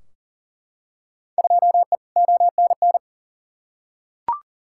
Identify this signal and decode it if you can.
Morse code, heard as a keyed tone, about 37 wpm, 700 Hz. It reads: 2E ONN
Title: Field Day Exchanges Encore - 40wpm